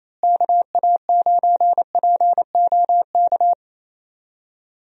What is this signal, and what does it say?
Morse code, heard as a keyed tone, about 28 wpm, 700 Hz. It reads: KA9POK
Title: USA Call Signs - 6 Characters - Repeated 3x - 28wpm